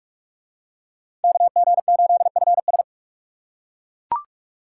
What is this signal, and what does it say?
Morse code, heard as a keyed tone, about 40 wpm, 700 Hz. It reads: KG8FS